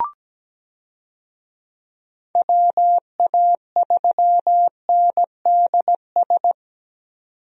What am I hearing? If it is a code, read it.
Morse code, heard as a keyed tone, about 17 wpm, 700 Hz. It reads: WA3NDS
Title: USA Call Signs - 6 Characters - 17wpm